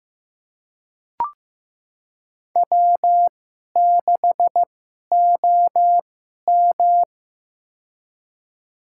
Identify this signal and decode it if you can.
Morse code, heard as a keyed tone, about 15 wpm, spaced as for 11 wpm, 700 Hz. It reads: W6OM
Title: USA Call Signs - 4 Characters - 2x Character Spacing - 15wpm